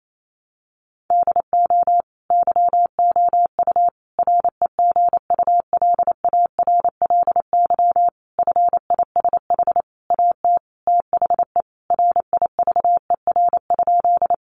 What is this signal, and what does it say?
Morse code, heard as a keyed tone, about 28 wpm, 700 Hz. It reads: DO YOU REGULARLY FISH AT THE RIVER?
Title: Sentences from Top 1400 Words 28wpm